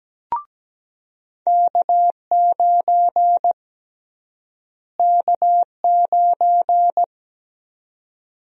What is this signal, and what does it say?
Morse code, heard as a keyed tone, about 17 wpm, 700 Hz. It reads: K9 K9
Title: USA Call Sign Prefixes - Repeated 2x - 17wpm